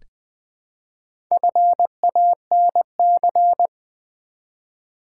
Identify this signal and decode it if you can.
Morse code, heard as a keyed tone, about 20 wpm, 700 Hz. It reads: FANC
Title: Build A Word - 1 Letter At A Time - 5-Letter Words - 20wpm